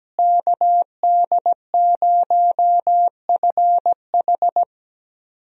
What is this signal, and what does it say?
Morse code, heard as a keyed tone, about 17 wpm, 700 Hz. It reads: KD0FH
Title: USA Call Signs - 5 Characters - Repeated 2x - 17wpm